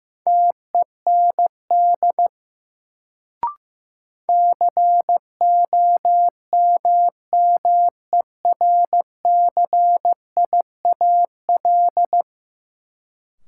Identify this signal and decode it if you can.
Morse code, heard as a keyed tone, about 15 wpm, 700 Hz. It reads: TEND COMMERCIAL
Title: Sentences from Top 1000 Words - Review Part 5 - New Word Drill - 15wpm